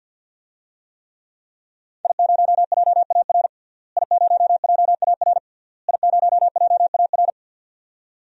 Morse code, heard as a keyed tone, about 52 wpm, 700 Hz. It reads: I0JAR I0JAR I0JAR
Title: DXCC-100 Call Signs - 5 Characters - Repeated 3x - 50wpm